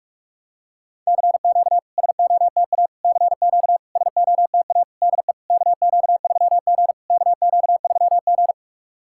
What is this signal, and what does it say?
Morse code, heard as a keyed tone, about 46 wpm, 690 Hz. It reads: CQ SOTA CQ SOTA DE KQ3G KQ3G
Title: SOTA QSO Practice 45wpm